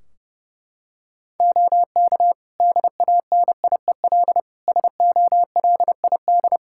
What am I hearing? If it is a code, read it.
Morse code, heard as a keyed tone, about 30 wpm, 700 Hz. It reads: OK DANIEL SOLID
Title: K1USN SST QSO Practice 30wpm